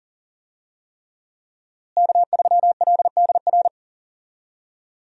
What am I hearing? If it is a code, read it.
Morse code, heard as a keyed tone, about 40 wpm, 700 Hz. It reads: K3LDR
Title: USA Call Signs - 5 Characters - Repeated 3x - 40wpm